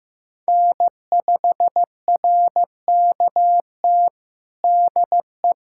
Morse code, heard as a keyed tone, about 16 wpm, 700 Hz. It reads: N5RKT DE
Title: USA Callsign DE Callsign - 5 Characters - 15wpm